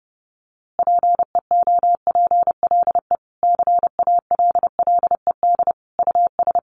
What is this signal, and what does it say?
Morse code, heard as a keyed tone, about 30 wpm, 700 Hz. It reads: PEOPLE CALLED US
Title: A Sound Of Thunder 30wpm